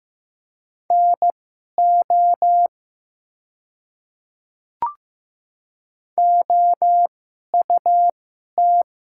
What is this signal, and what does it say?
Morse code, heard as a keyed tone, about 15 wpm, spaced as for 11 wpm, 700 Hz. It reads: NO OUT
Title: 2x Character Spacing - Top 100 Words - 15wpm